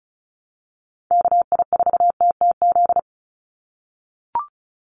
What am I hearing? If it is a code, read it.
Morse code, heard as a keyed tone, about 34 wpm, 700 Hz. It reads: KI4TTZ
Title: USA Call Signs 35wpm